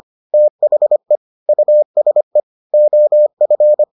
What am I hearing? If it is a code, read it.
Morse code, heard as a keyed tone, about 25 wpm, 600 Hz. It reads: THE USE OF